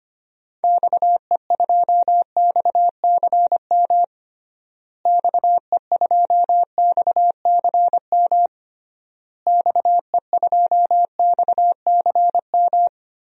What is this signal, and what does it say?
Morse code, heard as a keyed tone, about 25 wpm, 700 Hz. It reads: XE2XCM XE2XCM XE2XCM
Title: DXCC-100 Call Signs - 6 Characters - Repeated 3x - 25wpm